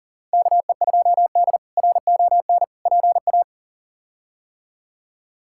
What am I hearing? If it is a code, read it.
Morse code, heard as a keyed tone, about 40 wpm, 700 Hz. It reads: KE2D RON PA
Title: K1USN SST QSO Practice - 40wpm